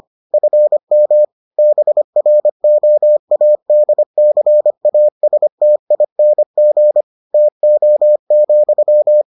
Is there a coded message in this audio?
Yes: Morse code from 0.3 to 9.2 s, heard as a keyed tone, about 25 wpm, 600 Hz. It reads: FM BROADCASTING TO,